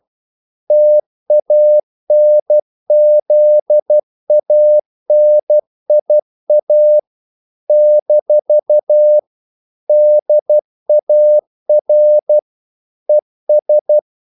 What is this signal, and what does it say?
Morse code, heard as a keyed tone, about 12 wpm, 600 Hz. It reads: TANZANIA - DAR ES
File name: CW / Practice / NationalCapitals12